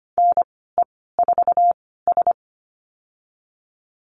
Morse code, heard as a keyed tone, about 25 wpm, spaced as for 16 wpm, 700 Hz. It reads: NE4S
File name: Lesson 15g - Number 5 - Call Signs 25wpm